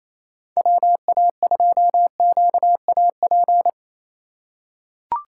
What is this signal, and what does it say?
Morse code, heard as a keyed tone, about 28 wpm, 700 Hz. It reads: WA2QAP
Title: USA Call Signs - 6 Characters - 4x Character Spacing - 28wpm